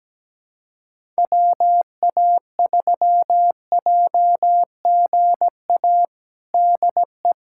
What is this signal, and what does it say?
Morse code, heard as a keyed tone, about 17 wpm, 700 Hz. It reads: WA3JGA DE
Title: USA Callsign DE Callsign 17wpm